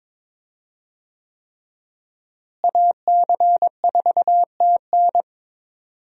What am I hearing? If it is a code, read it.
Morse code, heard as a keyed tone, about 22 wpm, 700 Hz. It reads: AC4TN
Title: USA Call Signs - 5 Characters - Repeated 3x - 22wpm